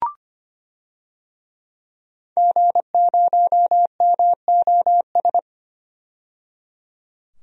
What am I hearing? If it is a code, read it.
Morse code, heard as a keyed tone, about 25 wpm, 700 Hz. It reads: G0MOS
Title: DXCC-100 Call Signs - 5 Characters - 25wpm